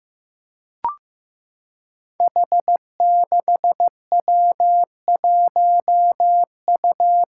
Morse code, heard as a keyed tone, about 15 wpm, 700 Hz. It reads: H6W1U
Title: Five Challenging Letter-Number Combinations 15wpm